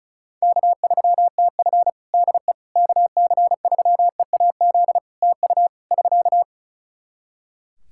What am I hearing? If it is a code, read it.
Morse code, heard as a keyed tone, about 34 wpm, 700 Hz. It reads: K3TF DE KC3EAZ TU <SK>
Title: Practice QSO - Exchange 4 - Ending - 35wpm